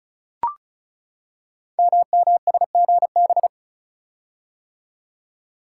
Morse code, heard as a keyed tone, about 36 wpm, 700 Hz. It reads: MMSGB
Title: Code Groups - 5 Letters - 35wpm